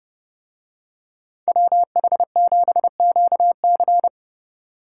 Morse code, heard as a keyed tone, about 31 wpm, 700 Hz. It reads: WH7QC